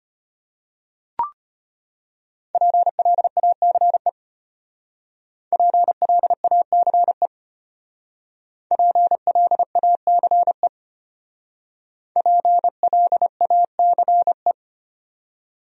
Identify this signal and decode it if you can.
Morse code, heard as a keyed tone, about 32 wpm, 700 Hz. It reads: PLACE PLACE PLACE PLACE